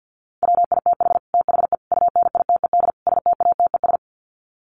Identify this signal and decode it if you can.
Morse code, heard as a keyed tone, about 50 wpm, 700 Hz. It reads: WITH THE UNITED STATES
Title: Sets of 4 Words 50wpm